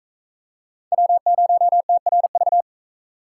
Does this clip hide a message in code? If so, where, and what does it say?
Yes: Morse code from 0.9 to 2.6 s, heard as a keyed tone, about 38 wpm, 700 Hz. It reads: W0TRV